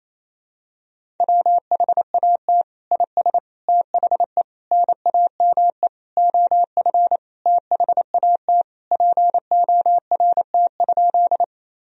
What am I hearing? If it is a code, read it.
Morse code, heard as a keyed tone, about 28 wpm, 700 Hz. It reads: WHAT IS THE NAME OF THAT PORT?
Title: Sentences from Top 1000 Words - Review Part 1 - 28wpm